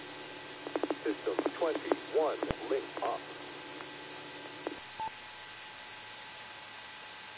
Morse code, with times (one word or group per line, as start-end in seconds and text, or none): none